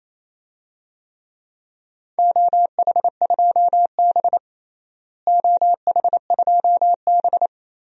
Morse code, heard as a keyed tone, about 28 wpm, 700 Hz. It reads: OH2B OH2B
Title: DXCC-100 Call Signs - 4 Characters - Repeated 3x - 28wpm